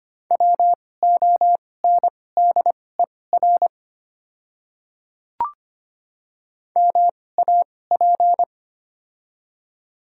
Morse code, heard as a keyed tone, about 25 wpm, spaced as for 18 wpm, 690 Hz. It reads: WONDER MAP
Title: 2x Character Spacing - Top 500 Words - 25wpm